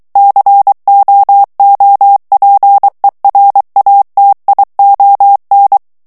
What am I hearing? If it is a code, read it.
Morse code, heard as a keyed tone, about 23 wpm, 800 Hz. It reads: COOPERATION
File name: ANGLAIS texte clair 3 vitesse 22 mots minute